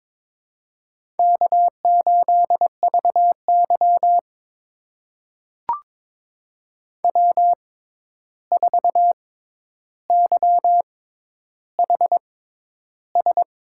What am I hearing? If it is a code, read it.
Morse code, heard as a keyed tone, about 22 wpm, spaced as for 11 wpm, 700 Hz. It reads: K8VY W4YHS
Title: USA Call Signs - 6x Character Spacing - 22wpm